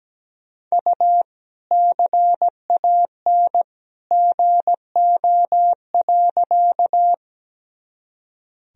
Morse code, heard as a keyed tone, about 17 wpm, 700 Hz. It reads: U CAN GO.
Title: Sentences from Top 100 Words and Abbreviations 17wpm